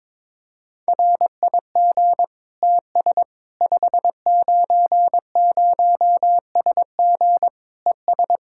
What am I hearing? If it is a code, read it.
Morse code, heard as a keyed tone, about 21 wpm, 700 Hz. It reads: RIG TS 590SG ES
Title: Practice QSO - Exchange 2 - Info - 22wpm